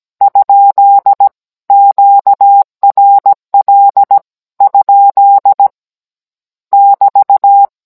Morse code, heard as a keyed tone, about 17 wpm, 800 Hz. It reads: ? QRL ? <BT>